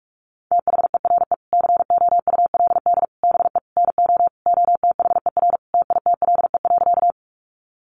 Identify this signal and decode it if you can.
Morse code, heard as a keyed tone, about 44 wpm, 700 Hz. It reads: THERE COULD BE NO OTHER TITLE.
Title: Sentences from Top 1200 Words 45wpm